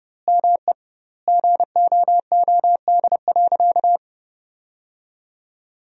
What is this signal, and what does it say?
Morse code, heard as a keyed tone, about 30 wpm, 700 Hz. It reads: ME GOOD.
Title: 2x Word-Spacing - Sentences from Top 500 Words - 30wpm